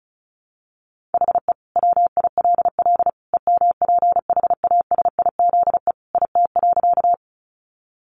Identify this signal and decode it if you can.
Morse code, heard as a keyed tone, about 34 wpm, 700 Hz. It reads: HE WILL EMPHASIZE IT.
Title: Sentences from Top 1500 Words 35wpm